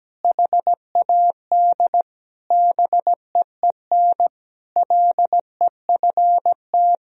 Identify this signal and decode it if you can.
Morse code, heard as a keyed tone, about 17 wpm, 700 Hz. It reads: HAD BEEN LEFT
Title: Sentences from Top 1000 Words - Review Part 1 - 17wpm